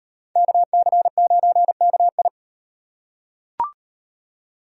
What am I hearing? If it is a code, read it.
Morse code, heard as a keyed tone, about 36 wpm, 700 Hz. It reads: KC9KI